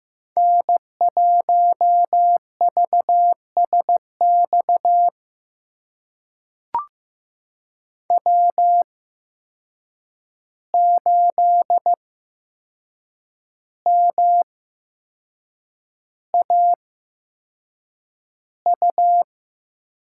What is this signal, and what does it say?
Morse code, heard as a keyed tone, about 15 wpm, spaced as for 6 wpm, 700 Hz. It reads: N1VSX W8MAU
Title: USA Call Signs - 5 Characters - 8x Character Spacing - 15wpm